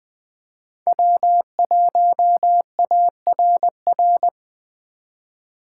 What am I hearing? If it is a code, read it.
Morse code, heard as a keyed tone, about 20 wpm, 700 Hz. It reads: W1ARR
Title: 2023 POTA Hunter Call Signs - 20wpm